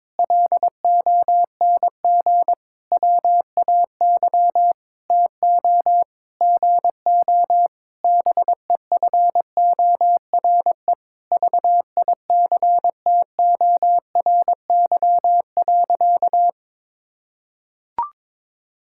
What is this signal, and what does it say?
Morse code, heard as a keyed tone, about 22 wpm, 700 Hz. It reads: LONG WAY TO GO BEFORE VICTORY.
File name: Sentences with 9-Word Phrases - 22wpm